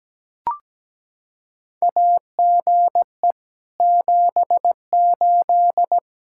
Morse code, heard as a keyed tone, about 17 wpm, 700 Hz. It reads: AGE 78
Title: Practice QSO - Exchange 3 - More Info - 17wpm